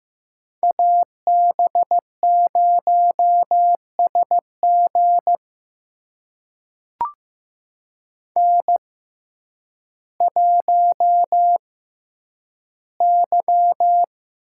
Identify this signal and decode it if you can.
Morse code, heard as a keyed tone, about 15 wpm, spaced as for 9 wpm, 700 Hz. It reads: AB0SG N1Y
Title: USA Call Signs - 5 Characters - 6x Character Spacing - 15wpm